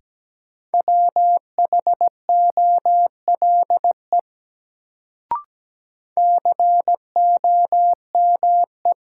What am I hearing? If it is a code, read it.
Morse code, heard as a keyed tone, about 17 wpm, 700 Hz. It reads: WHOLE COME
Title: Sets of 2 Words - New Word Drill - 17wpm